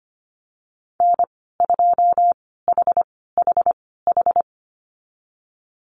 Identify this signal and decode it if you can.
Morse code, heard as a keyed tone, about 25 wpm, spaced as for 16 wpm, 700 Hz. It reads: N2HHH
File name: Lesson 15g - Number 5 - Call Signs 25wpm